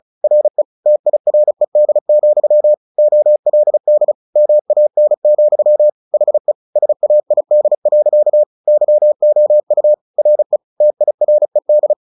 Morse code, heard as a keyed tone, about 34 wpm, 600 Hz. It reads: RE TIRED, OLD MAN, HE SAID. YOU RE TIRED
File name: the old man and the sea 2 35wpm